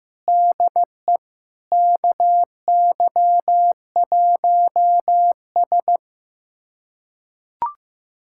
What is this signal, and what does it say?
Morse code, heard as a keyed tone, about 15 wpm, 700 Hz. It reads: DE KY1S
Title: USA Callsign DE Callsign - 4 Characters - 15wpm